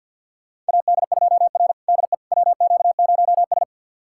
Morse code, heard as a keyed tone, about 52 wpm, 700 Hz. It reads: AD2R DE WQ0S
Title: USA Callsign DE Callsign - 4 Characters - 50wpm